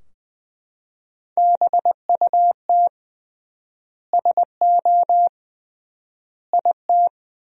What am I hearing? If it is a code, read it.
Morse code, heard as a keyed tone, about 20 wpm, 700 Hz. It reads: BUT SO IT
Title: 3x Word-Spacing - Sentences from Top 100 Words - 20wpm